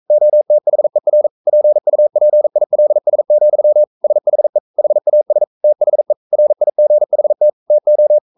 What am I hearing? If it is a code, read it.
Morse code, heard as a keyed tone, about 42 wpm, 600 Hz. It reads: OTHER PUPILS, SHE HAS THE RIGHT TO
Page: Romanen Never af Ken Follet på engelsk.